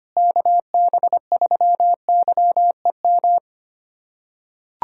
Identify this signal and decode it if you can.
Morse code, heard as a keyed tone, about 25 wpm, 700 Hz. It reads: KB3YEM
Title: USA Callsign DE Callsign 25wpm